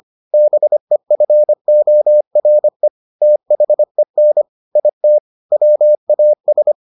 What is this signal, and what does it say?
Morse code, heard as a keyed tone, about 25 wpm, 600 Hz. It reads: BEFORE THEN IT WAS